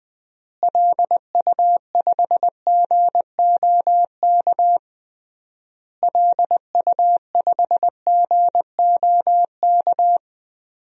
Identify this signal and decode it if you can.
Morse code, heard as a keyed tone, about 21 wpm, 700 Hz. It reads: LU5GOK LU5GOK
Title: DXCC-100 Call Signs - 6 Characters - Repeated 3x - 20wpm